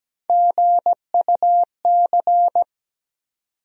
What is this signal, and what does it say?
Morse code, heard as a keyed tone, about 17 wpm, 700 Hz. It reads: GUC
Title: USA Call Sign Postfixes - 3 Letters 17wpm